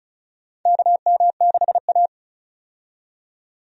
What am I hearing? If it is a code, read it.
Morse code, heard as a keyed tone, about 34 wpm, 700 Hz. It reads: KM6A
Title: USA Call Signs - 4 Characters - 35wpm